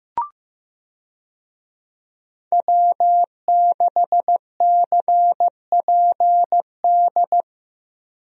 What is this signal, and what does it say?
Morse code, heard as a keyed tone, about 15 wpm, 700 Hz. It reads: W6CPD
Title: USA Call Signs - 5 Characters - Repeated 3x - 15wpm